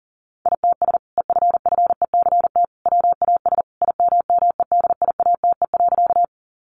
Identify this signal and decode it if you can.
Morse code, heard as a keyed tone, about 37 wpm, 700 Hz. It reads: ITS EFFECT WAS IMMEDIATE.
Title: Sentences from Top 1200 Words 40wpm